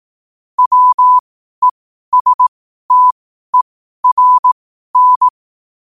Morse code, heard as a keyed tone, about 18 wpm, spaced as for 12 wpm, 1000 Hz. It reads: WESTERN
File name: qotdfiles / 12wpm